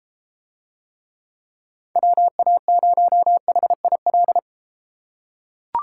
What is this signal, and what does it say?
Morse code, heard as a keyed tone, about 33 wpm, 700 Hz. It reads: WA0HIL